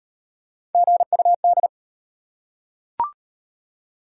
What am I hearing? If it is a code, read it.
Morse code, heard as a keyed tone, about 38 wpm, 700 Hz. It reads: GUD